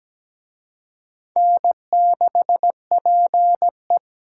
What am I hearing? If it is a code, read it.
Morse code, heard as a keyed tone, about 20 wpm, 700 Hz. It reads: N6PE
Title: Speed Racing - USA Call Signs 17wpm